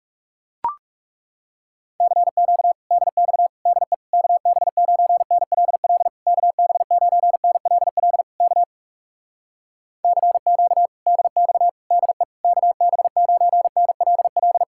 Morse code, heard as a keyed tone, about 40 wpm, 700 Hz. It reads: CQ DX DE KB9NLL KB9NLL K CQ DX DE KB9NLL